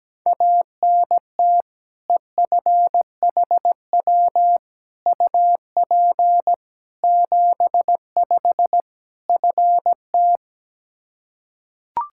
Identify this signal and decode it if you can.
Morse code, heard as a keyed tone, about 17 wpm, 700 Hz. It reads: ANT EFHW UP 75 FT
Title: K1USN SST QSO Practice 17wpm